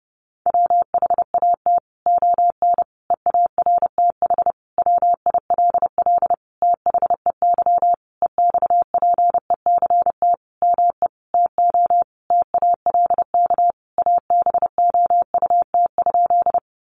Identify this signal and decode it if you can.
Morse code, heard as a keyed tone, about 30 wpm, 700 Hz. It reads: WHAT ON EARTH WILL THEY EXPECT ME TO TALK ABOUT?
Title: Sentences from Top 1100 Words 30wpm